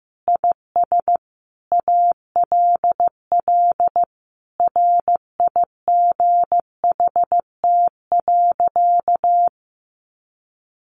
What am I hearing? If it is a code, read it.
Morse code, heard as a keyed tone, about 15 wpm, 700 Hz. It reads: IS ALL RIGHT.
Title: Sentences from Top 1200 Words 15wpm